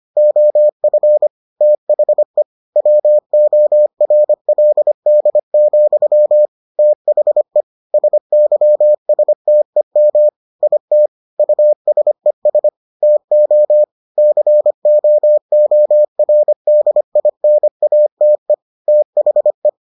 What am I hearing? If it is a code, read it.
Morse code, heard as a keyed tone, about 25 wpm, 600 Hz. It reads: OF THE WORLD, THE SYSTEM IT USES TO COORDINATE THE